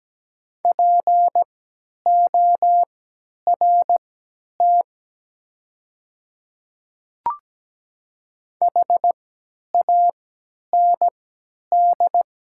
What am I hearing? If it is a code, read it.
Morse code, heard as a keyed tone, about 17 wpm, spaced as for 10 wpm, 700 Hz. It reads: PORT HAND
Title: 3x Character Spacing - Top 200 Words - 17wpm